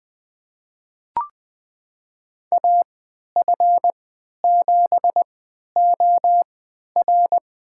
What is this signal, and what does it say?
Morse code, heard as a keyed tone, about 20 wpm, spaced as for 11 wpm, 700 Hz. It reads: AF7OR
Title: USA Call Signs - 3x Character Spacing - 20wpm